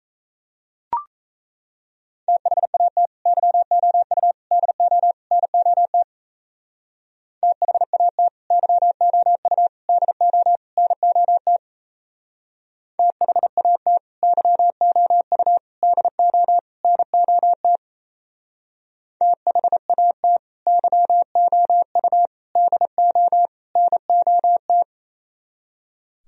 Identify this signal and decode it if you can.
Morse code, heard as a keyed tone, about 35 wpm, 700 Hz. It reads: THAT YOU DO NOT THAT YOU DO NOT THAT YOU DO NOT THAT YOU DO NOT
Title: Speed-Racing - Sets of 4 Words 28wpm